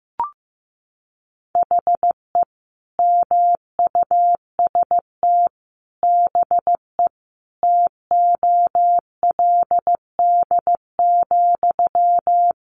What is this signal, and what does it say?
Morse code, heard as a keyed tone, about 15 wpm, 700 Hz. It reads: HE MUST BE TOLD,
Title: Sentences from Top 1400 Words 15wpm